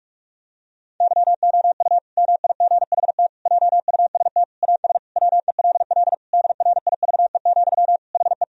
Morse code, heard as a keyed tone, about 46 wpm, 700 Hz. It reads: YOU MIGHT JUST AS WELL DRIVE, HE
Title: Sentences from Top 1000 Words - Review Part 2 - 45wpm